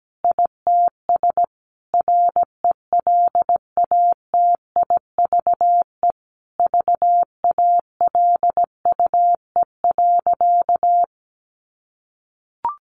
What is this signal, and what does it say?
Morse code, heard as a keyed tone, about 17 wpm, 700 Hz. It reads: ITS RELATIVE VALUE.
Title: Sentences from Top 1400 Words 17wpm